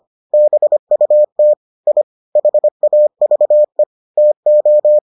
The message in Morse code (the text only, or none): BUT I HAVE TO